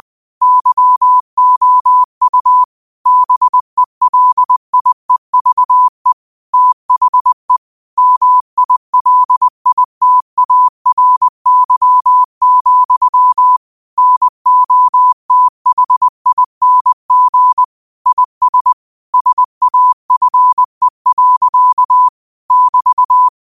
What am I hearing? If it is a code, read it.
Morse code, heard as a keyed tone, about 20 wpm, 1000 Hz. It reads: YOU BELIEVE THE MILITARY, NOTHING IS SAFE. <BT>
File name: qotdfiles / 20wpm